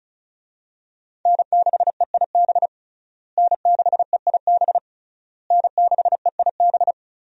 Morse code, heard as a keyed tone, about 34 wpm, 700 Hz. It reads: N6EIB N6EIB N6EIB
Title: USA Call Signs - 5 Characters - Repeated 3x - 35wpm